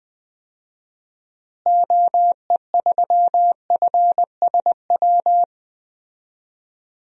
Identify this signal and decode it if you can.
Morse code, heard as a keyed tone, about 20 wpm, 700 Hz. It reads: OE3FSW
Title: DXCC-100 Call Signs - 6 Characters - Repeated 3x - 20wpm